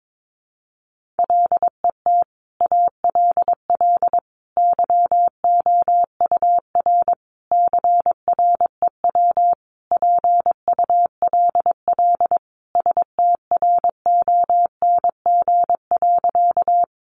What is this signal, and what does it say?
Morse code, heard as a keyed tone, about 22 wpm, 700 Hz. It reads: LET ALL YOUR CREW PULL STRONG.
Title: Sentences from Top 1500 Words 22wpm